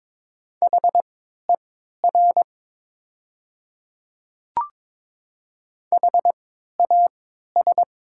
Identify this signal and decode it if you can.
Morse code, heard as a keyed tone, about 22 wpm, spaced as for 13 wpm, 700 Hz. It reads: HER HAS
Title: 3x Character Spacing - Top 100 Words - 22wpm